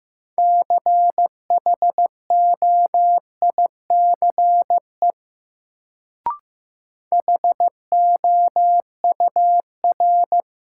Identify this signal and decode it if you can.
Morse code, heard as a keyed tone, about 15 wpm, 700 Hz. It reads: CHOICE HOUR
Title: Sets of 5 Words - New Word Drill - 15wpm